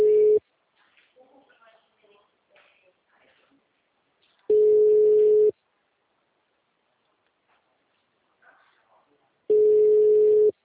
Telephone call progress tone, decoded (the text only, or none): ringback tone